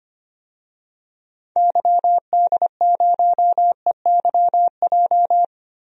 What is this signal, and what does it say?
Morse code, heard as a keyed tone, about 25 wpm, 700 Hz. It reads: YD0EYJ
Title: DXCC-100 Call Signs - 6 Characters - Repeated 2x - 25wpm